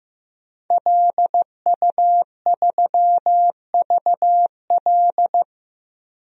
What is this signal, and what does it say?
Morse code, heard as a keyed tone, about 15 wpm, 700 Hz. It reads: LU3VL
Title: DXCC-100 Call Signs - 5 Characters - Repeated 2x - 15wpm